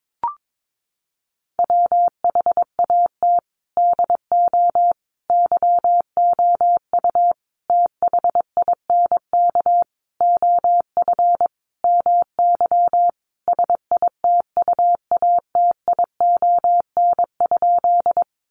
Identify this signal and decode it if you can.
Morse code, heard as a keyed tone, about 22 wpm, 700 Hz. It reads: WHAT DO YOU THINK OF MY SITUATION?
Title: Sentences from Top 1200 Words 22wpm